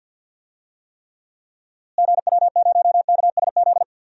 Morse code, heard as a keyed tone, about 46 wpm, 700 Hz. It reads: GW0KSZ
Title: DXCC-100 Call Signs - 6 Characters - Repeated 3x - 50wpm